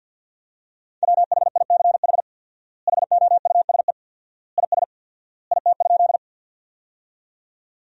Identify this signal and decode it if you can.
Morse code, heard as a keyed tone, about 50 wpm, 700 Hz. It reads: WHICH HOUSE IS IT?
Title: 4x Word-Spacing - Sentences from Top 200 Words - 50wpm